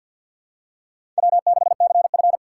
Morse code, heard as a keyed tone, about 51 wpm, 700 Hz. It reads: W6CF
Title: USA Call Signs - 4 Characters - 3x Character Spacing - 50wpm